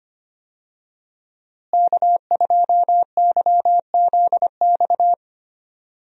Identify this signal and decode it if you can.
Morse code, heard as a keyed tone, about 25 wpm, 700 Hz. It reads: K2YZX
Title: USA Call Signs - 5 Characters - Repeated 3x - 25wpm